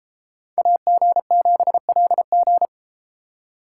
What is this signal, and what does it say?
Morse code, heard as a keyed tone, about 31 wpm, 700 Hz. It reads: AG7LG